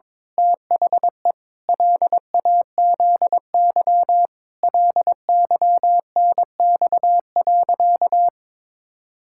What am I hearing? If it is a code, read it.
Morse code, heard as a keyed tone, about 22 wpm, 700 Hz. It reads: THE LAZY LYNX.
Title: Pangram Challenge v2 - 22wpm